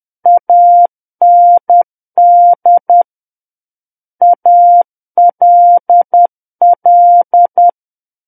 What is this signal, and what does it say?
Morse code, heard as a keyed tone, about 10 wpm, 700 Hz. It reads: AND ALL
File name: Book One / 01-Chapter 1 - 10 WPM